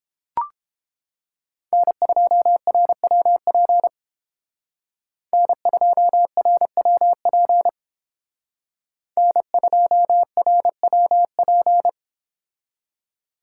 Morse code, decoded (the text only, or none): N2RWP N2RWP N2RWP